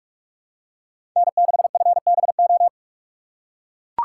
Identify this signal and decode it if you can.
Morse code, heard as a keyed tone, about 43 wpm, 700 Hz. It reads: N6FBO
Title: USA Call Signs - 5 Characters - 2x Character Spacing - 45wpm